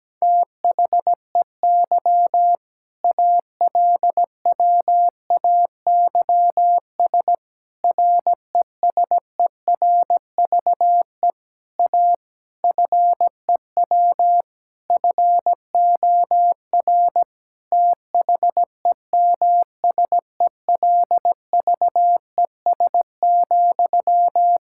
Morse code, heard as a keyed tone, about 17 wpm, 700 Hz. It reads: THEY ALWAYS RESERVE A FEW FOR THEMSELVES,
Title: Sentences from Top 1000 Words - Review Part 6 - 17wpm